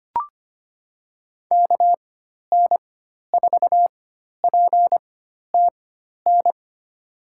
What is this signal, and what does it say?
Morse code, heard as a keyed tone, about 25 wpm, spaced as for 12 wpm, 700 Hz. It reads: KN4PTN
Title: USA Call Signs - 6 Characters - 4x Character Spacing - 25wpm